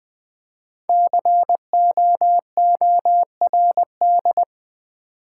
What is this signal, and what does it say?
Morse code, heard as a keyed tone, about 20 wpm, 700 Hz. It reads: COORD
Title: Build A Word - 5 Letters At A Time - 10-Letter Words - 20wpm